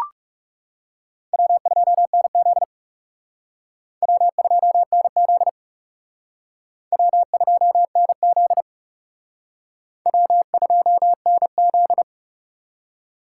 Morse code, heard as a keyed tone, about 36 wpm, 700 Hz. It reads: W2NZ W2NZ W2NZ W2NZ
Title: Speed Racing - USA Call Signs 30wpm